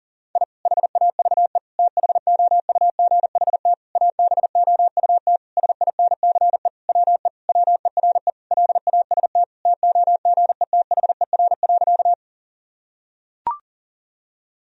Morse code, heard as a keyed tone, about 37 wpm, 700 Hz. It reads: I HAVE THOUGHT ABOUT SINCE WE WERE LAST TOGETHER.
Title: Sentences from Top 1000 Words - Review Part 1 - 40wpm